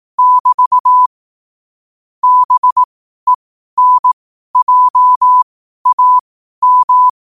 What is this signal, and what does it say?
Morse code, heard as a keyed tone, about 18 wpm, spaced as for 12 wpm, 1000 Hz. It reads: <BT> BENJAM
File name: qotdfiles / 12wpm